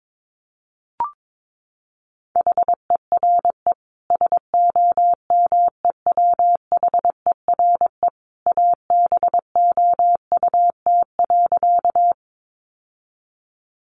Morse code, heard as a keyed tone, about 22 wpm, 700 Hz. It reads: HERE SOMEWHERE ABOUT.
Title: Sentences from Top 1400 Words 22wpm